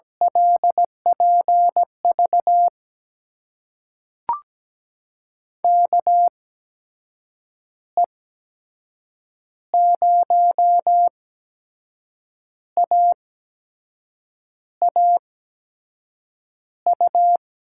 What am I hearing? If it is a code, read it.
Morse code, heard as a keyed tone, about 17 wpm, spaced as for 6 wpm, 700 Hz. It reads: LPV KE0AAU
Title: USA Call Signs - 6 Characters - 8x Character Spacing - 17wpm